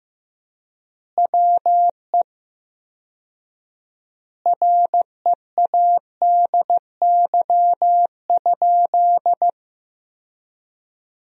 Morse code, heard as a keyed tone, about 15 wpm, 700 Hz. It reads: WE READY?
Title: 4x Word-Spacing - Sentences from Top 400 Words - 15wpm